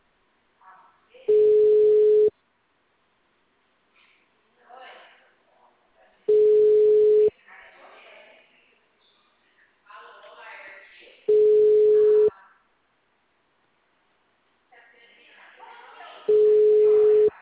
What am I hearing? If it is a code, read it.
Ringback tone (Europe, 425 Hz), with speech.
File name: out- -2044- - - .10076